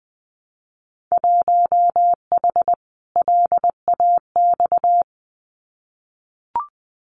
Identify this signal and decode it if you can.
Morse code, heard as a keyed tone, about 20 wpm, 700 Hz. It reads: 1H LAX
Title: Winter Field Day Exchanges 20wpm